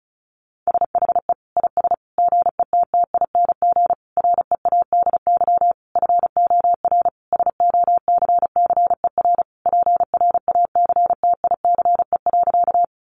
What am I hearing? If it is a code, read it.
Morse code, heard as a keyed tone, about 34 wpm, 700 Hz. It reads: SHE IS GETTING READY FOR SOCCER PRACTICE.
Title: Easy Phrases 35wpm